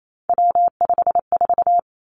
Morse code, heard as a keyed tone, about 28 wpm, 700 Hz. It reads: W54
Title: Three Challenging Letter-Number Combinations 28wpm